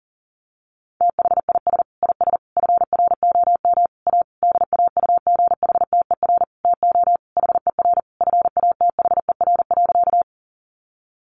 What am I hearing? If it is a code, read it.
Morse code, heard as a keyed tone, about 40 wpm, 700 Hz. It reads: THIS IS FROM A DAUGHTER TO HER FATHER.
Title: Sentences from Top 1500 Words 40wpm